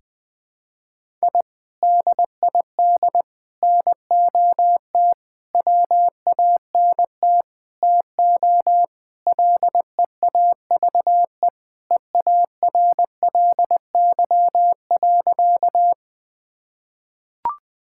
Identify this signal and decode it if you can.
Morse code, heard as a keyed tone, about 20 wpm, 700 Hz. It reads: I DID NOT WANT TO LEAVE EARLY.
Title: Sentences with 5-Word Phrases - Spoken Beginning - 20wpm